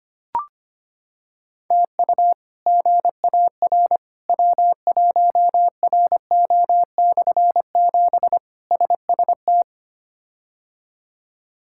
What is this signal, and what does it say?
Morse code, heard as a keyed tone, about 25 wpm, 700 Hz. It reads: TU GAR W1RO/7 SST
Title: K1USN SST QSO Practice - 25wpm